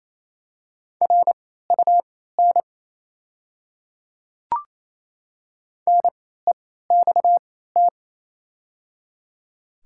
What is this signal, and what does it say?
Morse code, heard as a keyed tone, about 27 wpm, spaced as for 16 wpm, 700 Hz. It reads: RUN NEXT
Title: 3x Character Spacing - Top 300 Words - 28wpm